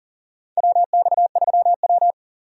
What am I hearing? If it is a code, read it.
Morse code, heard as a keyed tone, about 40 wpm, 700 Hz. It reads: WX3W
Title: USA Call Signs - 4 Characters - 40wpm